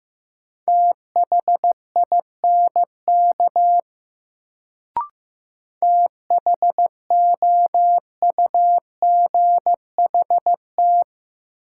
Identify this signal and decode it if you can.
Morse code, heard as a keyed tone, about 15 wpm, 700 Hz. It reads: THINK THOUGHT
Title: Sets of 2 Words - New Word Drill - 15wpm